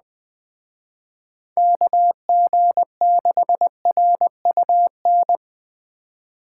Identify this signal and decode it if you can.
Morse code, heard as a keyed tone, about 20 wpm, 700 Hz. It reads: KG6RUN
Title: Speed Racing - USA 2x3 Call Signs 17wpm